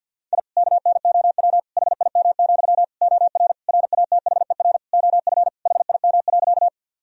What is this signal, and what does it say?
Morse code, heard as a keyed tone, about 50 wpm, 700 Hz. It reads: I KNOW HIM, OR RATHER OF HIM.
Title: Sentences from Top 1000 Words - Review Part 4 - 50wpm